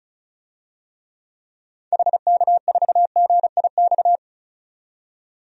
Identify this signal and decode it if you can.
Morse code, heard as a keyed tone, about 34 wpm, 700 Hz. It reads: HK4GIX
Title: DXCC-100 Call Signs - 6 Characters - 35wpm